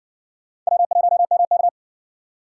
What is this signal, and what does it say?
Morse code, heard as a keyed tone, about 70 wpm, 700 Hz. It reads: W1MK